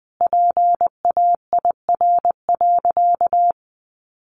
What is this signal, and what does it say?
Morse code, heard as a keyed tone, about 20 wpm, 700 Hz. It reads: PAIR.
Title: Sentences from Top 1200 Words 20wpm